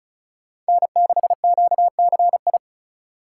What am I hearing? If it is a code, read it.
Morse code, heard as a keyed tone, about 34 wpm, 700 Hz. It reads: N6QCI
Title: USA Call Signs - 4x Character Spacing - 35wpm